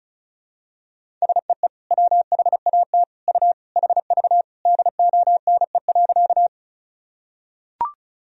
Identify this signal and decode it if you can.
Morse code, heard as a keyed tone, about 36 wpm, 700 Hz. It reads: SEE WHAT U HV DONE.
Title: Sentences from Top 500 Words and Abbreviations 35wpm